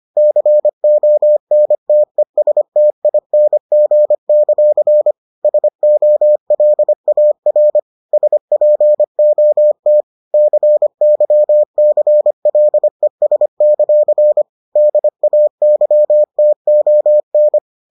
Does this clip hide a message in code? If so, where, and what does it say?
Yes: Morse code from 0.2 to 17.6 s, heard as a keyed tone, about 25 wpm, 600 Hz. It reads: CONTESTING; SOLAR SPOT CYCLES; DAYTON